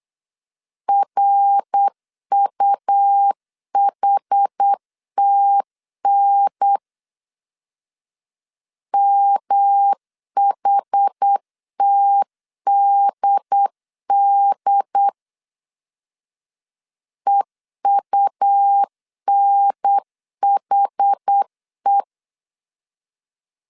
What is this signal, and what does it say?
Morse code, heard as a keyed tone, about 8 wpm, 800 Hz. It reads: RUHTN MHTDD EUNHE
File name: CW / DET / 1 3 1500C S B 8wpm 4141